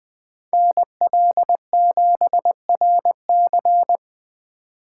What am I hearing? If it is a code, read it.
Morse code, heard as a keyed tone, about 23 wpm, 700 Hz. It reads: NL7RC
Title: Speed Racing - USA 2x2 Call Signs 20wpm